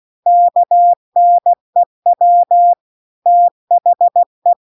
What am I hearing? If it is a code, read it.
Morse code, heard as a keyed tone, about 16 wpm, 700 Hz. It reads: KNEW THE
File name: CW-The-Big-Twoheaded-River-2-16WPM